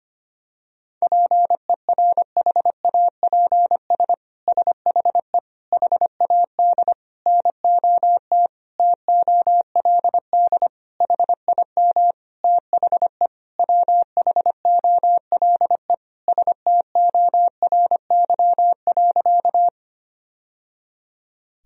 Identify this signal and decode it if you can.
Morse code, heard as a keyed tone, about 25 wpm, 700 Hz. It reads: PERHAPS SHE HAD NOT TOLD HIM THE WHOLE STORY.
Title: Sentences from Top 1000 Words - Review Part 1 - 25wpm